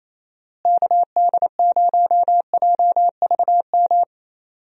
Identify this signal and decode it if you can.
Morse code, heard as a keyed tone, about 28 wpm, 700 Hz. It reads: KD0JVM
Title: USA Call Signs - 6 Characters - Repeated 2x - 28wpm